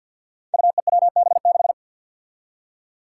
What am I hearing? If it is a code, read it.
Morse code, heard as a keyed tone, about 52 wpm, 700 Hz. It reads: UEWB6
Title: Five Challenging Letter-Number Combinations 50wpm